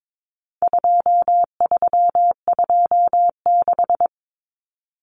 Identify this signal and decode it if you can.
Morse code, heard as a keyed tone, about 22 wpm, 700 Hz. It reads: 2326
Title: Common Cut Numbers - 4-Digit 22wpm